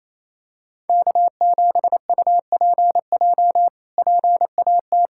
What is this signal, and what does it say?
Morse code, heard as a keyed tone, about 27 wpm, 700 Hz. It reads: K7UPJ PAT
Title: K1USN SST QSO Practice - 28wpm